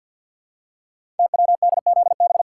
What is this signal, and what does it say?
Morse code, heard as a keyed tone, about 48 wpm, 690 Hz. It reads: TWDZB
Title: Code Groups - 5 Letters - 50wpm